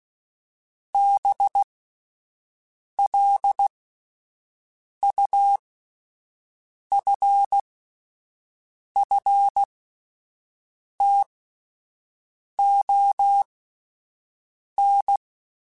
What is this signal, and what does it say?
Morse code, heard as a keyed tone, about 16 wpm, spaced as for 5 wpm, 800 Hz. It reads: BLUFFTON
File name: fromSMs / FromSMs mp3s / Cities@5